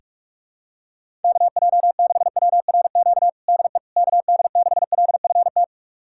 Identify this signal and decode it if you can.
Morse code, heard as a keyed tone, about 41 wpm, 700 Hz. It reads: KJ6WRQ DE KD6LFT
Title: USA Callsign DE Callsign - 6 Characters - 45wpm